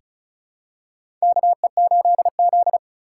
Morse code, heard as a keyed tone, about 34 wpm, 690 Hz. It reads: KE8Z
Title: USA Call Signs - 4x Character Spacing - 35wpm